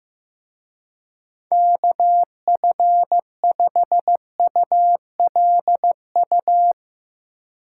Morse code, heard as a keyed tone, about 16 wpm, 700 Hz. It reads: KF5ULU
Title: Speed Racing - USA 2x3 Call Signs 15wpm